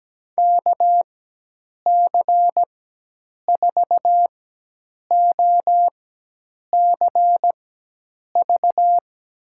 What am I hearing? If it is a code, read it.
Morse code, heard as a keyed tone, about 17 wpm, spaced as for 8 wpm, 700 Hz. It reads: KC4OCV
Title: USA Call Signs - 6 Characters - 4x Character Spacing - 17wpm